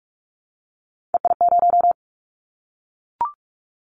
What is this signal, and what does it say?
Morse code, heard as a keyed tone, about 46 wpm, 700 Hz. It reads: EI0
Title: Callsign Prefixes for Top 50 DXCC - 45wpm